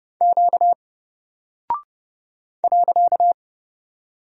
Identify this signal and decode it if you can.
Morse code, heard as a keyed tone, about 30 wpm, 700 Hz. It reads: Q .
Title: Single Character (Letter:Number:Punctuation:Prosign) 30wpm